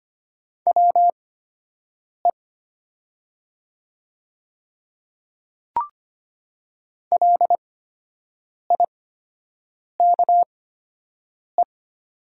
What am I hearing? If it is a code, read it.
Morse code, heard as a keyed tone, about 25 wpm, spaced as for 7 wpm, 700 Hz. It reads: WE LIKE
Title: 8x Character Spacing - Top 200 Words - 25wpm